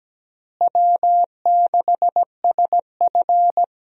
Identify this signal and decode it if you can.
Morse code, heard as a keyed tone, about 17 wpm, 700 Hz. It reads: W6SF
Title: USA Call Signs - 4 Characters - Repeated 3x - 17wpm